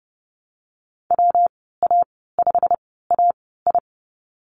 Morse code, heard as a keyed tone, about 30 wpm, spaced as for 17 wpm, 700 Hz. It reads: WA5AI
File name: Lesson 19g - Letter w - Call Signs 30wpm